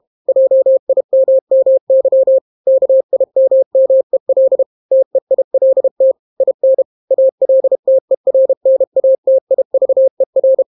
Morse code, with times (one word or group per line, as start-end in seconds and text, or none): 0.3-2.4 JIMMY
2.7-4.6 KIMMEL
4.9-6.1 TEILT
6.4-6.8 IN
7.1-10.6 ALTERNATIVER